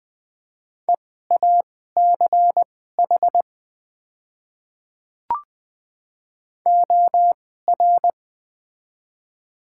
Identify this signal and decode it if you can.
Morse code, heard as a keyed tone, about 20 wpm, spaced as for 14 wpm, 700 Hz. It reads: EACH OR
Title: 2x Character Spacing - Top 200 Words - 20wpm